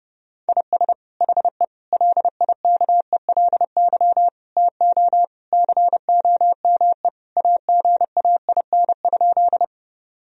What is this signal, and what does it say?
Morse code, heard as a keyed tone, about 32 wpm, 700 Hz. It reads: IS HE LIKELY TO COME AGAIN?
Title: Sentences from Top 1000 Words - Review Part 4 - 30wpm